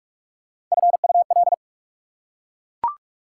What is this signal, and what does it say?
Morse code, heard as a keyed tone, about 40 wpm, 690 Hz. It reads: FUL